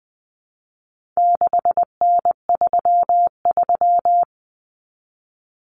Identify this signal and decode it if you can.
Morse code, heard as a keyed tone, about 20 wpm, 700 Hz. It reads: 6N33
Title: Common Cut Numbers - 4-Digit 20wpm